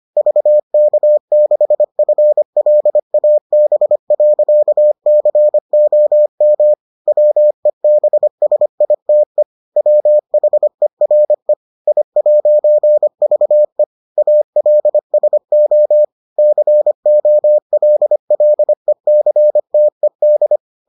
Morse code, heard as a keyed tone, about 25 wpm, 600 Hz. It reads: VK6FLAB.COM WEBSITE WHERE I'VE ALSO COLLECTED